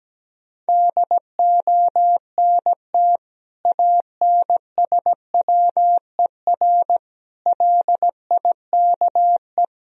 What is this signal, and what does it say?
Morse code, heard as a keyed tone, about 17 wpm, 700 Hz. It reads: DONT ANSWER LIKE
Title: Sentences from Top 400 Words and Abbreviations 17wpm